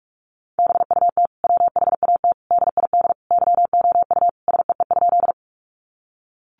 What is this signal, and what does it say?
Morse code, heard as a keyed tone, about 44 wpm, 700 Hz. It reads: BUT WHAT DID YOU SEE?
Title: Sentences from Top 100 Words 45wpm